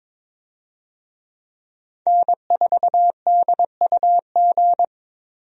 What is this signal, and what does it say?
Morse code, heard as a keyed tone, about 21 wpm, 700 Hz. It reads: N4DUG
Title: USA Call Signs - 5 Characters - 22wpm